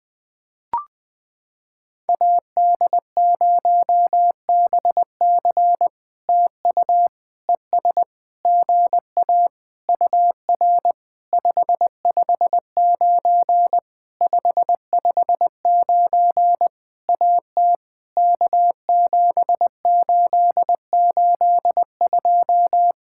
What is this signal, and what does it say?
Morse code, heard as a keyed tone, about 20 wpm, 700 Hz. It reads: AD0BC TU ES GA UR 559 559 AT K7882
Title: POTA QSO Practice - 20wpm